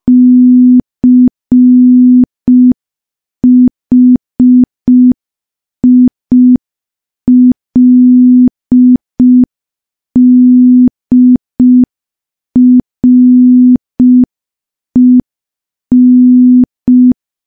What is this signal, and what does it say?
Morse code, heard as a keyed tone, about 5 wpm, 250 Hz. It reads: CHILDREN
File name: morse / morse-training-05WPM